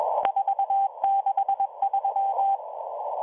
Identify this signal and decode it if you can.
Morse code, heard as a keyed tone, about 21 wpm, 780 Hz. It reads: 463